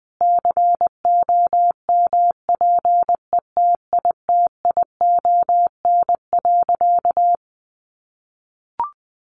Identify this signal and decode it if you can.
Morse code, heard as a keyed tone, about 20 wpm, 700 Hz. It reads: COMPETITION.
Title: Additional Easy Phrases 20wpm